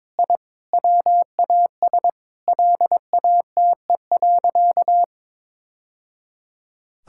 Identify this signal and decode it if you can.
Morse code, heard as a keyed tone, about 22 wpm, 700 Hz. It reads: I WAS LATE.
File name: practice / 22wpm / Sentences with 3-Word Phrases - 22wpm